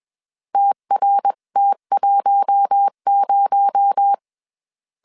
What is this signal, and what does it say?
Morse code, heard as a keyed tone, about 21 wpm, 800 Hz. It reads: TRT10